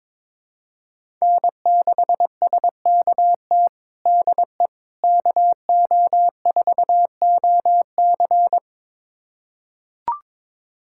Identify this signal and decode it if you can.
Morse code, heard as a keyed tone, about 22 wpm, 700 Hz. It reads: N6SKT DE KO4OC
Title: USA Callsign DE Callsign - 5 Characters - 22wpm